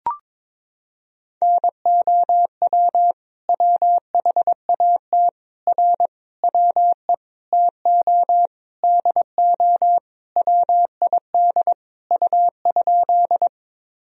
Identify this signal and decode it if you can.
Morse code, heard as a keyed tone, about 22 wpm, 700 Hz. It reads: NOW WHAT R WE TO DO WID U?
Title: Sentences from Top 100 Words and Abbreviations 22wpm